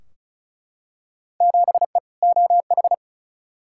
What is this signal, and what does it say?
Morse code, heard as a keyed tone, about 36 wpm, 700 Hz. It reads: 7E OH
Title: Field Day Exchanges Encore - 35wpm